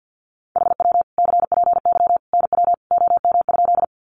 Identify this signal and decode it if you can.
Morse code, heard as a keyed tone, about 50 wpm, 700 Hz. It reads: HW CPY NW OM?
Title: Common QSO Phrases 50wpm